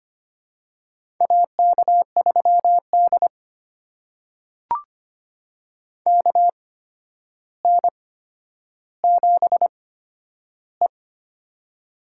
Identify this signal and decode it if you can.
Morse code, heard as a keyed tone, about 25 wpm, spaced as for 11 wpm, 700 Hz. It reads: AK3D KN7E
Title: USA Call Signs - 4 Characters - 8x Character Spacing - 25wpm